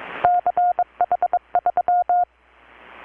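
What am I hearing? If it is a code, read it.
Morse code, heard as a keyed tone, about 22 wpm, 720 Hz. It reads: CH3
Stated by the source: Morse from KPH's Night of Nights XX maritime radiotelegraphy event, received in Chiapas, Mexico about 3,800 km away on a Kenwood R-600 with a grounded 10 m long-wire antenna.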